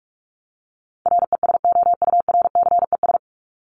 Morse code, heard as a keyed tone, about 44 wpm, 700 Hz. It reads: RESOURCES
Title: ES:S Suffix - Top 100 Words with Base Words 45wpm